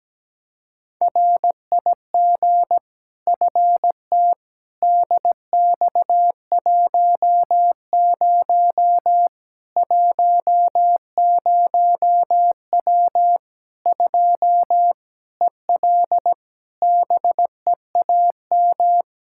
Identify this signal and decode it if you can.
Morse code, heard as a keyed tone, about 17 wpm, 700 Hz. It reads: RIG FT DX10 10W 2 EL BEAM
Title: Practice QSO - Exchange 2 - Info - 17wpm